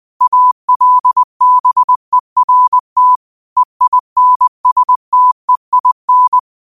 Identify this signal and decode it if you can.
Morse code, heard as a keyed tone, about 20 wpm, 1000 Hz. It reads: ALBERT EINSTEIN